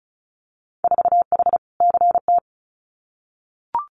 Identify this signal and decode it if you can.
Morse code, heard as a keyed tone, about 36 wpm, 700 Hz. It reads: 4H CT
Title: Winter Field Day Exchanges Encore 35wpm